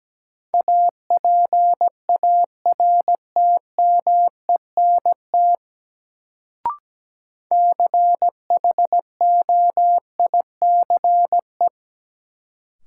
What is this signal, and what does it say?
Morse code, heard as a keyed tone, about 17 wpm, 700 Hz. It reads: APARTMENT CHOICE
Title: Sentences from Top 900 Words - New Word Drill - 17wpm